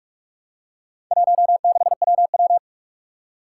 Morse code, heard as a keyed tone, about 44 wpm, 700 Hz. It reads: 16WW
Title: Four Challenging Letter-Number Combinations 45wpm